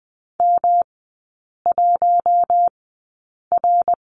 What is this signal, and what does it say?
Morse code, heard as a keyed tone, about 20 wpm, spaced as for 8 wpm, 690 Hz. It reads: M1R
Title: 145 Lesson 24e - Letter y - Three Characters 20wpm